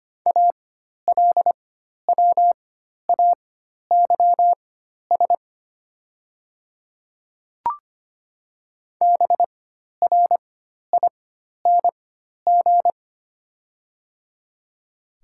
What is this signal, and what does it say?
Morse code, heard as a keyed tone, about 25 wpm, spaced as for 12 wpm, 690 Hz. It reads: ALWAYS BRING
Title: 4x Character Spacing - Top 500 Words - 25wpm